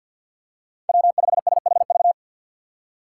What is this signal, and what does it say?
Morse code, heard as a keyed tone, about 47 wpm, 700 Hz. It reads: W5SHV
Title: USA Call Signs - 5 Characters - 6x Character Spacing - 50wpm